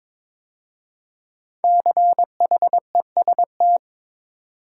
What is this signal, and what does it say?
Morse code, heard as a keyed tone, about 23 wpm, 700 Hz. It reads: CHEST
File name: 5 Letter Words - First 3 Letters Given - 22wpm